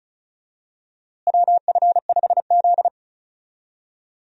Morse code, heard as a keyed tone, about 34 wpm, 700 Hz. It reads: WF5Z
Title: USA Call Signs - 4 Characters - Repeated 3x - 35wpm